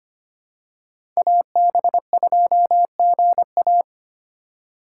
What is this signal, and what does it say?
Morse code, heard as a keyed tone, about 25 wpm, 700 Hz. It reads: AB2GA
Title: USA Call Signs - 5 Characters - 3x Character Spacing - 25wpm